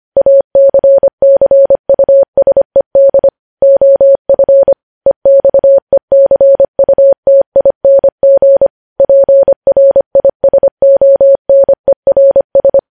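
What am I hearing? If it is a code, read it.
Morse code, heard as a keyed tone, about 25 wpm, 550 Hz. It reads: ACCUSED OF EXECUTING PRISONERS
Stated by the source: Morse code news headlines at 25 words per minute.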